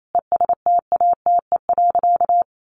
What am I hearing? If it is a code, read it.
Morse code, heard as a keyed tone, about 28 wpm, 700 Hz. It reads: ESTATE.
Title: Sentences from Top 1200 Words 28wpm